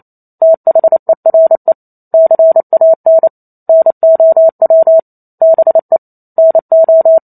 Morse code, heard as a keyed tone, about 28 wpm, 650 Hz. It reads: THERE CAN NOW BE NO